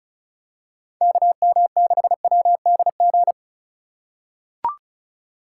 Morse code, heard as a keyed tone, about 36 wpm, 700 Hz. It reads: KM6WDG
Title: USA Call Signs - 6 Characters - 2x Character Spacing - 35wpm